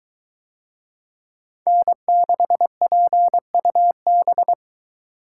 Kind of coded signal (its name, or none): Morse code